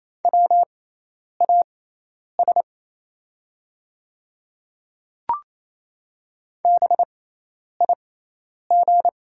Morse code, heard as a keyed tone, about 27 wpm, spaced as for 10 wpm, 700 Hz. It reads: WAS BIG
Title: 6x Character Spacing - Top 200 Words - 28wpm